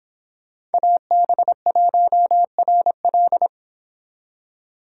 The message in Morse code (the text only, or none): AB1RL